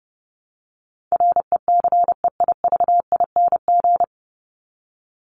Morse code, heard as a keyed tone, about 30 wpm, 700 Hz. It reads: RECEIVING
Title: ING Suffix - Top 100 Words with Base Words 30wpm